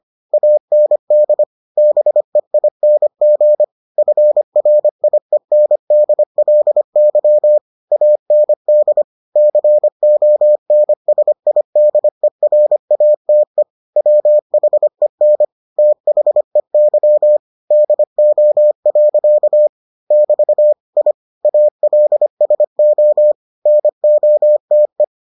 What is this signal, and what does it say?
Morse code, heard as a keyed tone, about 25 wpm, 600 Hz. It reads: AND BEING FRIENDLY AND CONSIDERATE WHEN THEY DO. <BT> I ALSO NOTE